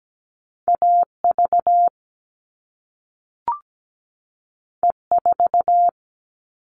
Morse code, heard as a keyed tone, about 17 wpm, 700 Hz. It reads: AV E4
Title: All Cut Numbers - 2-Digit 17wpm